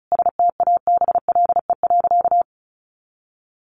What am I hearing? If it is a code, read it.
Morse code, heard as a keyed tone, about 34 wpm, 700 Hz. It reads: STABLE.
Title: Sentences from Top 1500 Words 35wpm